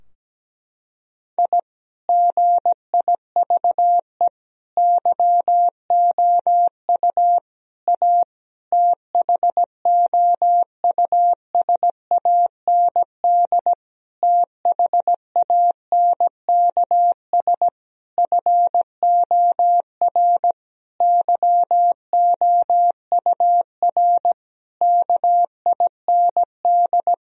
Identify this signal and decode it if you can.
Morse code, heard as a keyed tone, about 17 wpm, 700 Hz. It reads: I GIVE YOU A THOUSAND THANKS FOR YOUR KIND
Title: Sentences from Top 1000 Words - Review Part 4 - 17wpm